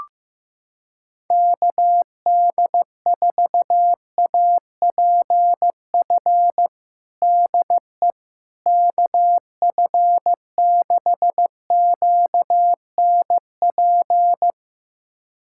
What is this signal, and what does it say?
Morse code, heard as a keyed tone, about 15 wpm, 700 Hz. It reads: KD4APF DE KF6QNP
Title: USA Callsign DE Callsign - 6 Characters - 15wpm